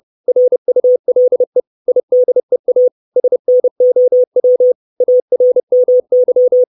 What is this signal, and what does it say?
Morse code, heard as a keyed tone, about 30 wpm, 500 Hz. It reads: RULE IDEA SNOW ARMY
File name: 632 Words 30WPM